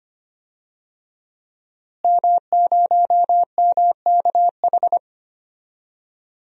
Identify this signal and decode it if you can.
Morse code, heard as a keyed tone, about 25 wpm, 700 Hz. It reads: M0MKH
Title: DXCC-100 Call Signs - 5 Characters - Repeated 2x - 25wpm